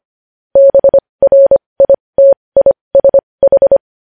Morse code, heard as a keyed tone, about 25 wpm, spaced as for 20 wpm, 550 Hz. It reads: BRITISH
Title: News Headlines at 20